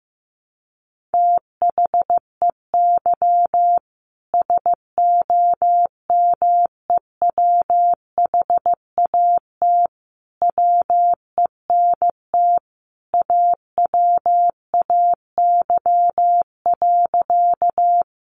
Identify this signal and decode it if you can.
Morse code, heard as a keyed tone, about 15 wpm, 700 Hz. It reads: THEY SOMEWHAT WENT AWAY.
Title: Sentences from Top 1300 Words 15wpm